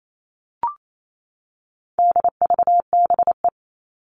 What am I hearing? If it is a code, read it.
Morse code, heard as a keyed tone, about 28 wpm, 700 Hz. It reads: DVBE
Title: All Cut Numbers - 4-Digit 28wpm